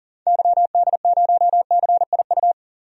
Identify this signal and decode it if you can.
Morse code, heard as a keyed tone, about 40 wpm, 700 Hz. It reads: YD0CIU
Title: DXCC-100 Call Signs - 6 Characters - 40wpm